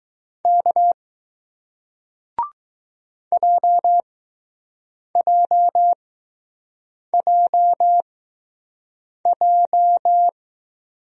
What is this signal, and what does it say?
Morse code, heard as a keyed tone, about 20 wpm, spaced as for 6 wpm, 700 Hz. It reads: K JJJJ